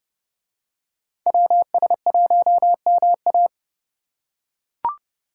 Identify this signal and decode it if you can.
Morse code, heard as a keyed tone, about 29 wpm, 700 Hz. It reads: WS1MA